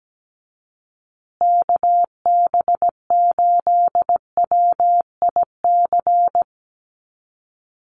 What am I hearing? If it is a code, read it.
Morse code, heard as a keyed tone, about 17 wpm, 700 Hz. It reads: KB8WIC
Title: USA Call Signs Repeated 2x 17wpm